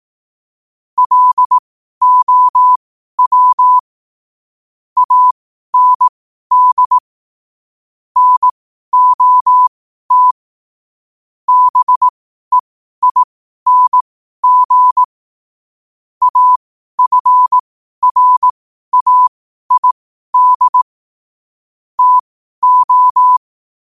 Morse code, heard as a keyed tone, about 18 wpm, spaced as for 12 wpm, 1000 Hz. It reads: LOW AND NOT BEING AFRAID TO
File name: qotdfiles / 12wpm